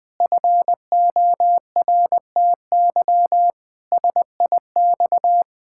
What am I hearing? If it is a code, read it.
Morse code, heard as a keyed tone, about 20 wpm, 700 Hz. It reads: FORTY SIX
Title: Pangram Challenge v2 - 20wpm